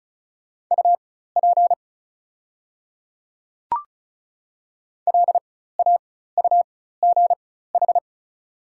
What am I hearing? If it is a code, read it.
Morse code, heard as a keyed tone, about 36 wpm, spaced as for 16 wpm, 700 Hz. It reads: UP LAUGH
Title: 4x Character Spacing - Top 500 Words - 35wpm